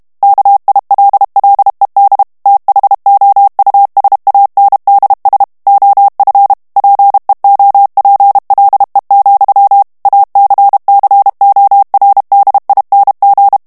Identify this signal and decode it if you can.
Morse code, heard as a keyed tone, about 32 wpm, 800 Hz. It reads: KILLED THOUSANDS OF PEOPLE, ACCORDING
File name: ANGLAIS texte clair 2 vitesse 30 mots minute